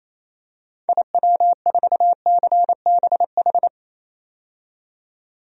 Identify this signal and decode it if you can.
Morse code, heard as a keyed tone, about 28 wpm, 700 Hz. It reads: IW4CBH
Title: DXCC-100 Call Signs - 6 Characters - 28wpm